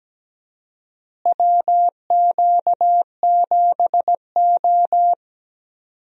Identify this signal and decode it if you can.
Morse code, heard as a keyed tone, about 17 wpm, 700 Hz. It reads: WQ7O
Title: USA Call Signs - 4 Characters - Repeated 2x - 17wpm